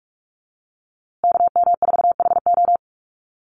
Morse code, heard as a keyed tone, about 44 wpm, 700 Hz. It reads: KM4HO
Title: USA Call Signs Repeated 3x 45wpm